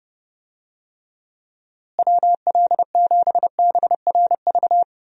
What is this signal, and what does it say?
Morse code, heard as a keyed tone, about 32 wpm, 700 Hz. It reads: WL7BRV
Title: USA Call Signs - 6 Characters - Repeated 2x - 30wpm